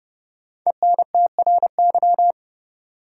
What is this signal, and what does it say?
Morse code, heard as a keyed tone, about 31 wpm, 700 Hz. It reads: ENTRY
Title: Sentences from Top 1500 Words - New Word Drill Alt - 30wpm